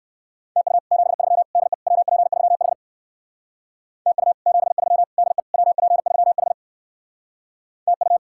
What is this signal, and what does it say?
Morse code, heard as a keyed tone, about 59 wpm, 700 Hz. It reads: TU 73 DE WP3H TU 73 DE WP3H TU